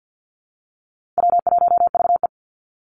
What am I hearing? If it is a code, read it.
Morse code, heard as a keyed tone, about 50 wpm, 700 Hz. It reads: W1VE
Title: Easy Call Signs Repeated 2x - 50wpm